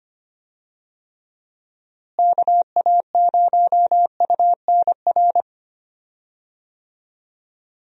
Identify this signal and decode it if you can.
Morse code, heard as a keyed tone, about 25 wpm, 700 Hz. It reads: KA0UNR
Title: USA Call Signs - 6 Characters - Repeated 2x - 25wpm